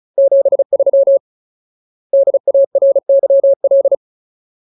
Morse code, heard as a keyed tone, about 35 wpm, 550 Hz. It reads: 73 DARYL
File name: Ultimate Word List / 35WPM Ultimate